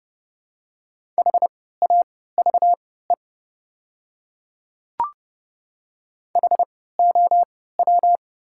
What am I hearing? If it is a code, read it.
Morse code, heard as a keyed tone, about 32 wpm, spaced as for 17 wpm, 700 Hz. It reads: HAVE HOW
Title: 3x Character Spacing - Top 100 Words - 30wpm